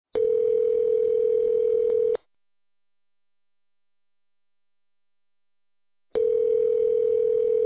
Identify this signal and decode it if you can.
Ringback tone (North America), 440 + 480 Hz.